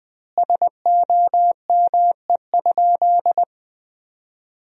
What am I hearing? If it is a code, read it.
Morse code, heard as a keyed tone, about 21 wpm, 700 Hz. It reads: SOME?
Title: 3x Word-Spacing - Sentences from Top 100 Words - 20wpm